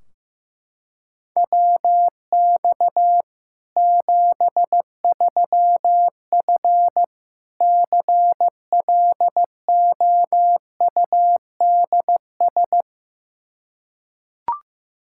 Morse code, heard as a keyed tone, about 16 wpm, 700 Hz. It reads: WX 73F CLOUDS
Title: Practice QSO - Exchange 2 - Info - 15wpm